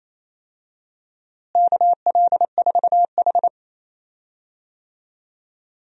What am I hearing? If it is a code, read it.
Morse code, heard as a keyed tone, about 28 wpm, 700 Hz. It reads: KL4H
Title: USA Call Signs - 4 Characters - Repeated 3x - 28wpm